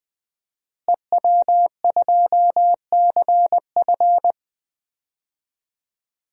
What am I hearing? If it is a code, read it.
Morse code, heard as a keyed tone, about 20 wpm, 700 Hz. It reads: EW2CF
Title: DXCC-100 Call Signs - 5 Characters - Repeated 3x - 20wpm